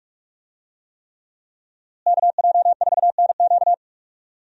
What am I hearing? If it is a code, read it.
Morse code, heard as a keyed tone, about 46 wpm, 700 Hz. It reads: KJ4NQ
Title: USA Call Signs - 5 Characters - Repeated 3x - 45wpm